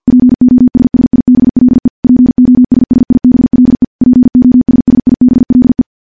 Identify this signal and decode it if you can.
Morse code, heard as a keyed tone, about 50 wpm, 250 Hz. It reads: POSSIBLE POSSIBLE POSSIBLE
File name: morse / morse-training-50WPM